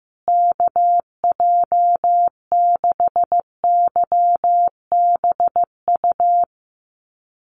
Morse code, heard as a keyed tone, about 15 wpm, 700 Hz. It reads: KJ6YBU
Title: USA Call Signs Repeated 2x 15wpm